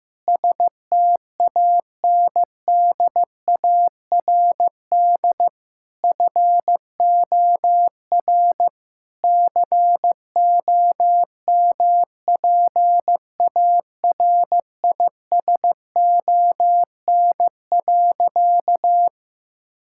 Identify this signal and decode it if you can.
Morse code, heard as a keyed tone, about 15 wpm, 700 Hz. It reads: STANDARD FOR COMPARISON.
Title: Sentences from Top 1600 Words 15wpm